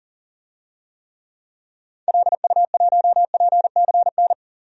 Morse code, heard as a keyed tone, about 37 wpm, 700 Hz. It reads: LU1PCN
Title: DXCC-100 Call Signs - 6 Characters - Repeated 3x - 40wpm